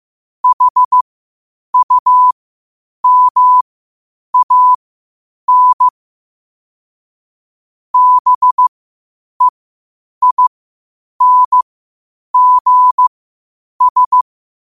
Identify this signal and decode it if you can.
Morse code, heard as a keyed tone, about 15 wpm, spaced as for 8 wpm, 1000 Hz. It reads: HUMAN BEINGS
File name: qotdfiles / 8wpm